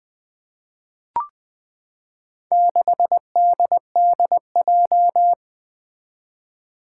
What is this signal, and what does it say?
Morse code, heard as a keyed tone, about 20 wpm, 700 Hz. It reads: 6DDJ
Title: Four Challenging Letter-Number Combinations 20wpm